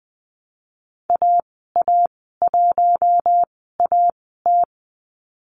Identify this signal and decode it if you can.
Morse code, heard as a keyed tone, about 20 wpm, spaced as for 15 wpm, 700 Hz. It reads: AA1AT
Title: 088 Lesson 15g - Number 5 - Call Signs 20wpm